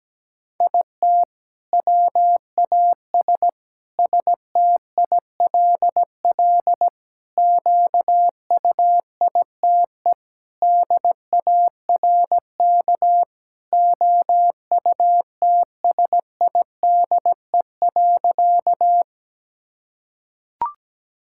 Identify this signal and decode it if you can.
Morse code, heard as a keyed tone, about 17 wpm, 700 Hz. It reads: IT WAS STILL QUITE DARK OUTSIDE.
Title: Sentences from Top 1000 Words - Review Part 6 - 17wpm